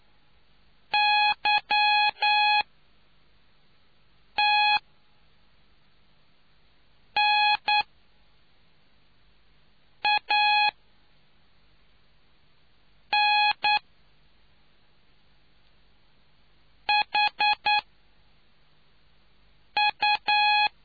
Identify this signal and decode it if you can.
Morse code, heard as a keyed tone, about 9 wpm, 840 Hz. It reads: Y T N A N H U